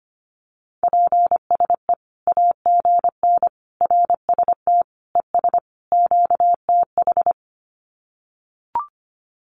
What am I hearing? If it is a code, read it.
Morse code, heard as a keyed tone, about 25 wpm, 700 Hz. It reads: PSE AGN RST ES QTH
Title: Common QSO Phrases 25wpm